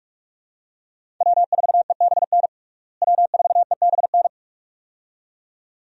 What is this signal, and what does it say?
Morse code, heard as a keyed tone, about 47 wpm, 700 Hz. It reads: W4EBN W4EBN
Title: USA Call Signs - 5 Characters - Repeated 2x - 45wpm